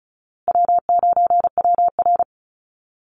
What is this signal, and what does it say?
Morse code, heard as a keyed tone, about 34 wpm, 700 Hz. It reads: W9WR
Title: Call Signs Repeated 2x - 35wpm